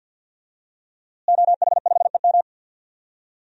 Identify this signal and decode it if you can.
Morse code, heard as a keyed tone, about 52 wpm, 700 Hz. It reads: OH5EM
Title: DXCC-100 Call Signs - 5 Characters - Repeated 3x - 50wpm